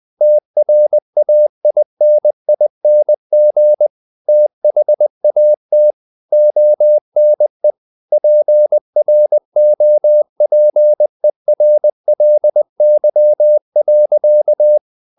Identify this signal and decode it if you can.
Morse code, heard as a keyed tone, about 20 wpm, 600 Hz. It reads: TRAINING THAT ONE PROPERLY.
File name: the old man and the sea part 2 20wpm